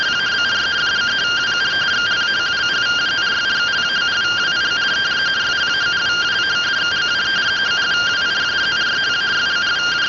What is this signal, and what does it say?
Two-tone teleprinter signal, about 170 Hz shift.